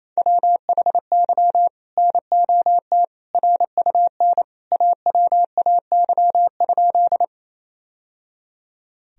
Morse code, heard as a keyed tone, about 28 wpm, 700 Hz. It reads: WHY NOT RUN AWAY?
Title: Sentences from Top 1000 Words - Review Part 1 - 28wpm